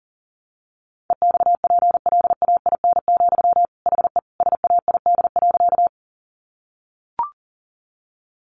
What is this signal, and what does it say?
Morse code, heard as a keyed tone, about 40 wpm, 700 Hz. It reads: EXPLAIN, HE SAID.
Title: Sentences from Top 1100 Words 40wpm